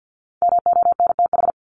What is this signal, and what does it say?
Morse code, heard as a keyed tone, about 50 wpm, 700 Hz. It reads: MONTH
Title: Top 501-750 Words 50wpm